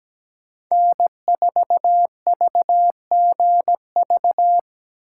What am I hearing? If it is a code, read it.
Morse code, heard as a keyed tone, about 17 wpm, 700 Hz. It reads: N4VGV
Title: USA Call Signs - 4x Character Spacing - 17wpm